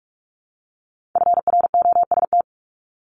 Morse code, heard as a keyed tone, about 46 wpm, 700 Hz. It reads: FROST
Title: Multi-letter Phonemes Repeated 3x 45wpm